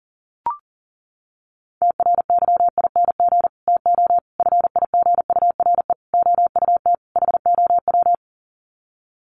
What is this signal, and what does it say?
Morse code, heard as a keyed tone, about 40 wpm, 700 Hz. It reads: TRYING TO FIGURE OUT HOW
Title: Sets of 5 Words Encore 40wpm